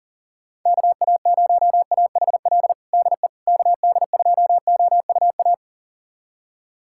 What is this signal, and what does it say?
Morse code, heard as a keyed tone, about 37 wpm, 700 Hz. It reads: KA0AHL DE KD2OUA
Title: USA Callsign DE Callsign 40wpm